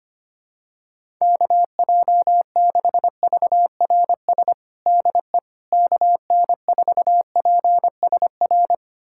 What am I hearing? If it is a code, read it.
Morse code, heard as a keyed tone, about 25 wpm, 700 Hz. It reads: KJ6VRS DE KN4PSR
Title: USA Callsign DE Callsign - 6 Characters - 25wpm